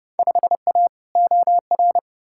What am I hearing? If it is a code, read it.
Morse code, heard as a keyed tone, about 30 wpm, 700 Hz. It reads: 5A OR
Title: Field Day Exchanges - 30wpm